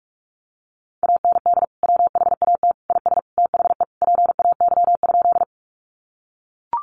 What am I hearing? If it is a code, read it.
Morse code, heard as a keyed tone, about 44 wpm, 700 Hz. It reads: AND WHAT IS THE PAY?
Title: Sentences from Top 700 Words 45wpm